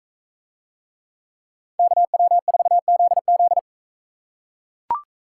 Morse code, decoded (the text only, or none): KW4ZZ